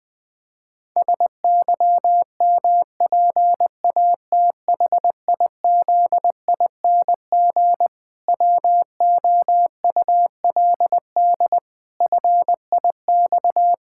Morse code, heard as a keyed tone, about 20 wpm, 700 Hz. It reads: SYMPATHIZING WOULD FIX
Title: Pangram Challenge v2 - 20wpm